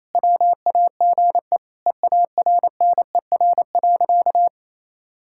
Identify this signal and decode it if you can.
Morse code, heard as a keyed tone, about 28 wpm, 700 Hz. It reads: WAGE EARNER.
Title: Pangram Challenge v2 - 28wpm